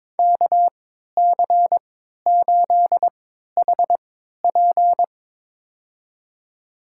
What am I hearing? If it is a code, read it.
Morse code, heard as a keyed tone, about 22 wpm, spaced as for 13 wpm, 700 Hz. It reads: KC8HP
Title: USA Call Signs - 5 Characters - 3x Character Spacing - 22wpm